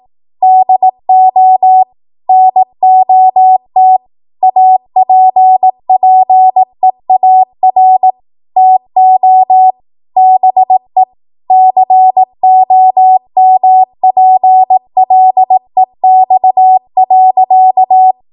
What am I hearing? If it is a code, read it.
Morse code, heard as a keyed tone, about 18 wpm, 750 Hz. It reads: DO NOT APPEAR TO BE COMPLEX.
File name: Morse / 18WPM